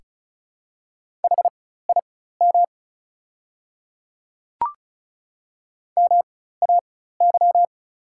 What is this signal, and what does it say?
Morse code, heard as a keyed tone, about 36 wpm, spaced as for 16 wpm, 700 Hz. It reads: HIM MAY
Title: 4x Character Spacing - Top 100 Words - 35wpm